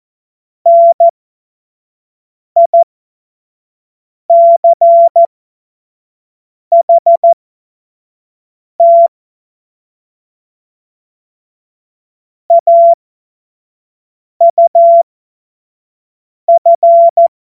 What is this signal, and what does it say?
Morse code, heard as a keyed tone, about 14 wpm, spaced as for 5 wpm, 680 Hz. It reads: NICHT AUF